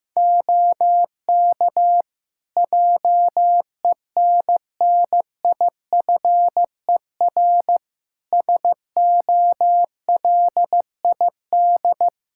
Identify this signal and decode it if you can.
Morse code, heard as a keyed tone, about 15 wpm, 700 Hz. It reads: OK JENNIFER SOLID
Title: Practice QSO - Exchange 3 - More Info - 15wpm